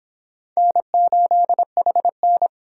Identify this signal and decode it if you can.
Morse code, heard as a keyed tone, about 26 wpm, 700 Hz. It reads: N8HN